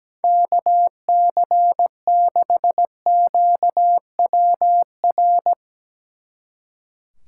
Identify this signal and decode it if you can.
Morse code, heard as a keyed tone, about 18 wpm, 700 Hz. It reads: KC6QWR
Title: Speed Racing - USA 2x3 Call Signs 17wpm